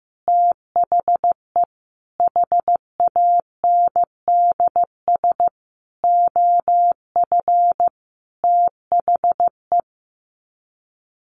Morse code, heard as a keyed tone, about 14 wpm, 700 Hz. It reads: THE HANDS OF THE
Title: Sets of 5 Words Encore2 15wpm